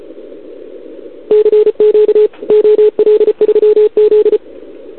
Morse code, heard as a keyed tone, about 34 wpm, 440 Hz, with radio noise. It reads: CQ OL3Z
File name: CQWWCW2021 / OL3Z